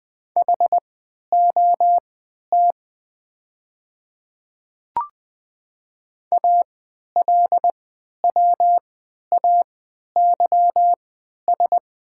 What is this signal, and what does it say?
Morse code, heard as a keyed tone, about 20 wpm, spaced as for 11 wpm, 700 Hz. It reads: HOT ALWAYS
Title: 3x Character Spacing - Top 500 Words - 20wpm